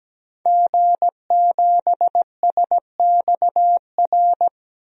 Morse code, heard as a keyed tone, about 17 wpm, 700 Hz. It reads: G7SXR
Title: DXCC-100 Call Signs - 5 Characters - 17wpm